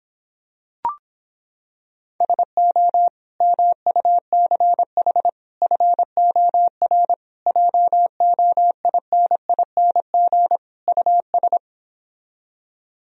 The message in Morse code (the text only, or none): SO MUCH FOR JOINING US